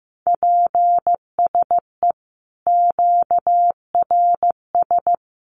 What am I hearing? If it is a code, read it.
Morse code, heard as a keyed tone, about 15 wpm, 700 Hz. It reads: PSE QRS
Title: Common QSO Phrases 15wpm